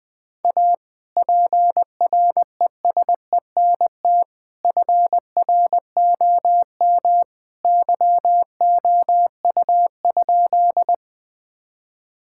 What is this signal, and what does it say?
Morse code, heard as a keyed tone, about 20 wpm, 700 Hz. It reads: A PRESENT FROM YOU?
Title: Sentences from Top 1000 Words - Review Part 5 - 20wpm